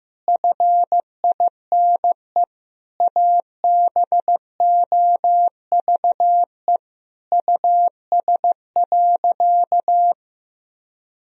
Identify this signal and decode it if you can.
Morse code, heard as a keyed tone, about 15 wpm, 700 Hz. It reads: FINE ABOVE US.
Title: Sentences from Top 1000 Words - Review Part 2 - 15wpm